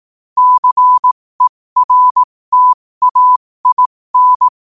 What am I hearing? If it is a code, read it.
Morse code, heard as a keyed tone, about 18 wpm, spaced as for 15 wpm, 1000 Hz. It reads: CERTAIN
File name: qotdfiles / 15wpm